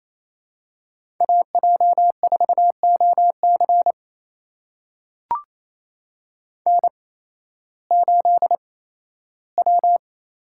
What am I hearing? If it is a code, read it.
Morse code, heard as a keyed tone, about 27 wpm, spaced as for 15 wpm, 700 Hz. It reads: AJ4OC N8W
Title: USA Call Signs - 5 Characters - 8x Character Spacing - 28wpm